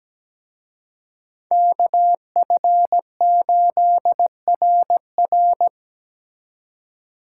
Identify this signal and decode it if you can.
Morse code, heard as a keyed tone, about 20 wpm, 700 Hz. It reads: KF8RR
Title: Speed Racing - USA 2x2 Call Signs 17wpm